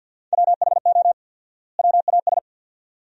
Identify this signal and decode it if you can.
Morse code, heard as a keyed tone, about 50 wpm, 700 Hz. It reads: WHO WAS
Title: 4x Word-Spacing - Sentences from Top 100 Words - 50wpm